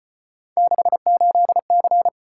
Morse code, heard as a keyed tone, about 33 wpm, 700 Hz. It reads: 68C